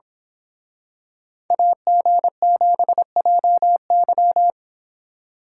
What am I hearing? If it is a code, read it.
Morse code, heard as a keyed tone, about 26 wpm, 700 Hz. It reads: AG7JY
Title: Speed Racing - USA 2x2 Call Signs 22wpm